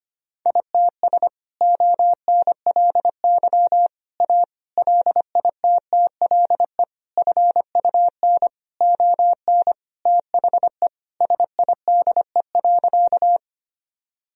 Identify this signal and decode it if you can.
Morse code, heard as a keyed tone, about 25 wpm, 700 Hz. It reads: ITS ONLY A LITTLE FUN ON THE SIDE.
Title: Sentences from Top 1000 Words - Review Part 4 - 25wpm